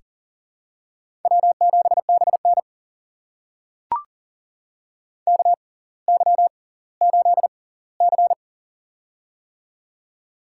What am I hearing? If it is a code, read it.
Morse code, heard as a keyed tone, about 40 wpm, spaced as for 21 wpm, 700 Hz. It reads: W7BN KY8C
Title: USA Call Signs - 4 Characters - 6x Character Spacing - 40wpm